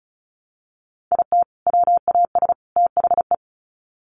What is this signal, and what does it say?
Morse code, heard as a keyed tone, about 34 wpm, 700 Hz. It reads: IT WAS THE
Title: Sets of 3 Words 35wpm